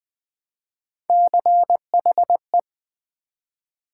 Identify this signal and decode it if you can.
Morse code, heard as a keyed tone, about 20 wpm, 700 Hz. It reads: CHE
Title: Build A Word - 1 Letter At A Time - 3 to 5-Letter Words - 20wpm